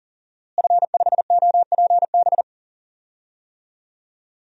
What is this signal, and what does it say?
Morse code, heard as a keyed tone, about 40 wpm, 700 Hz. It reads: F5OPB
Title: DXCC-100 Call Signs - 5 Characters - Repeated 2x - 40wpm